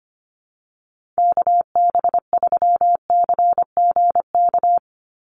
Morse code, heard as a keyed tone, about 25 wpm, 700 Hz. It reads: KB3CGK
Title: USA Call Signs 25wpm